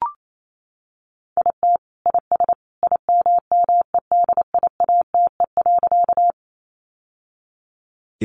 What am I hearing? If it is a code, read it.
Morse code, heard as a keyed tone, about 28 wpm, 700 Hz. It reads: IT IS IMMEDIATE.
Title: Sentences from Top 1200 Words 28wpm